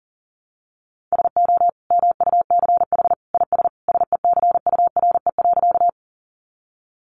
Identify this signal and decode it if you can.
Morse code, heard as a keyed tone, about 37 wpm, 700 Hz. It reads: SO MUCH IS SECURE.
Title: Sentences from Top 1200 Words 40wpm